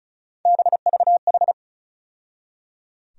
Morse code, heard as a keyed tone, about 36 wpm, 700 Hz. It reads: BVH
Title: USA Call Sign Postfixes - 3 Letters 35wpm